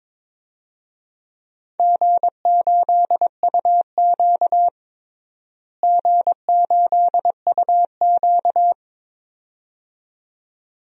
Morse code, heard as a keyed tone, about 21 wpm, 700 Hz. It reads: G8UQ G8UQ
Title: DXCC-100 Call Signs - 4 Characters - Repeated 2x - 22wpm